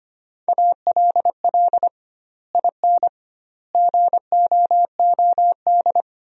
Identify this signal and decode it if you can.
Morse code, heard as a keyed tone, about 25 wpm, 700 Hz. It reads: ALL IN GOOD
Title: 2x Word-Spacing - Sentences from Top 200 Words - 25wpm